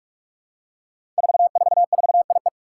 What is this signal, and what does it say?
Morse code, heard as a keyed tone, about 44 wpm, 700 Hz. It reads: 444IE
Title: Five Challenging Letter-Number Combinations 45wpm